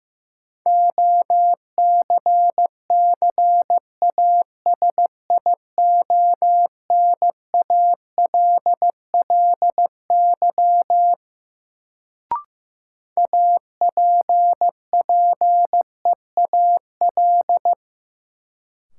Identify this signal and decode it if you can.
Morse code, heard as a keyed tone, about 15 wpm, 700 Hz. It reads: OCCASIONALLY APPEAL
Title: Sentences from Top 1300 Words - New Word Drill - 15wpm